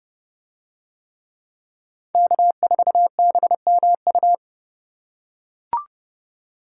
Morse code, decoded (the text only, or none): K4BMU